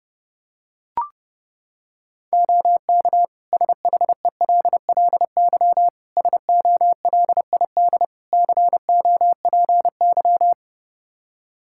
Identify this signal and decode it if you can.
Morse code, heard as a keyed tone, about 32 wpm, 700 Hz. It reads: OK SHELLY SOLID COPY
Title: Practice QSO - Exchange 3 - More Info - 30wpm